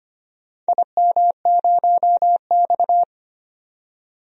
Morse code, heard as a keyed tone, about 25 wpm, 700 Hz. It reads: IM0X
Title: DXCC-100 Call Signs - 4 Characters - 25wpm